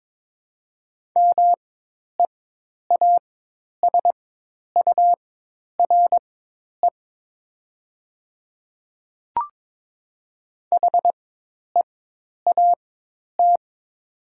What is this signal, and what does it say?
Morse code, heard as a keyed tone, about 22 wpm, spaced as for 10 wpm, 700 Hz. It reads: MEASURE HEAT
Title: 4x Character Spacing - Top 500 Words - 22wpm